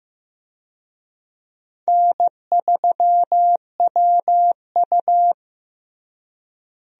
Morse code, heard as a keyed tone, about 15 wpm, 700 Hz. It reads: N3WU
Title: USA Call Signs - 4 Characters - Repeated 2x - 15wpm